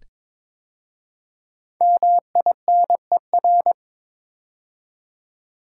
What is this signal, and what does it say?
Morse code, heard as a keyed tone, about 22 wpm, 700 Hz. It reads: MINER
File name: 5 Letter Words - First 3 Letters Given - 22wpm